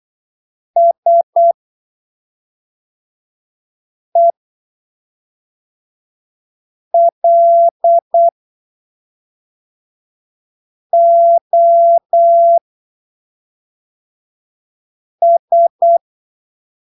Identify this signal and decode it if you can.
Morse code, heard as a keyed tone, about 8 wpm, spaced as for 3 wpm, 680 Hz. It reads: SELOS
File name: arquivos / texto / texto 16 08ppm